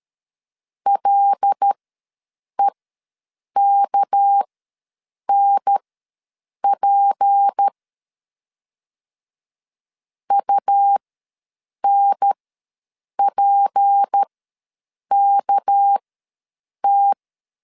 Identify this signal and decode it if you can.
Morse code, heard as a keyed tone, about 13 wpm, spaced as for 7 wpm, 800 Hz. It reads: LEKNP UNPKT
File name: LEZIONI / 6 PKX0 / 6 PK ws 12wpm 3116